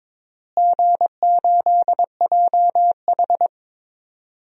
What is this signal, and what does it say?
Morse code, heard as a keyed tone, about 22 wpm, 700 Hz. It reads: G8JH
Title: DXCC-100 Call Signs - 4 Characters - Repeated 3x - 22wpm